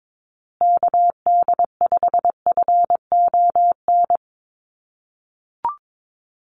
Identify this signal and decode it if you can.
Morse code, heard as a keyed tone, about 22 wpm, 700 Hz. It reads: KD5FON
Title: USA Call Signs Repeated 3x 22wpm